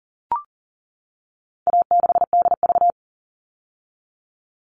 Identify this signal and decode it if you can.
Morse code, heard as a keyed tone, about 40 wpm, 700 Hz. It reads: A6DV
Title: All Cut Numbers - 4-Digit 40wpm